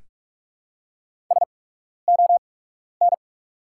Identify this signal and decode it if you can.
Morse code, heard as a keyed tone, about 45 wpm, spaced as for 12 wpm, 700 Hz. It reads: SON
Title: 8x Character Spacing - Top 500 Words - 45wpm